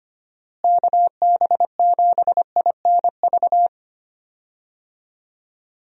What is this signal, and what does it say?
Morse code, heard as a keyed tone, about 25 wpm, 700 Hz. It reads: KB7INV
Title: USA Call Signs - 6 Characters - Repeated 2x - 25wpm